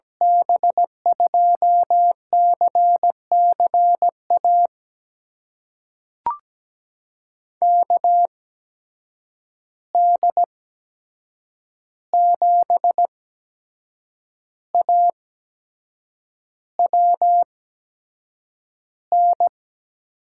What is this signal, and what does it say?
Morse code, heard as a keyed tone, about 17 wpm, spaced as for 7 wpm, 700 Hz. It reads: B2CCA KD7AWN
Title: USA Call Signs - 8x Character Spacing - 17wpm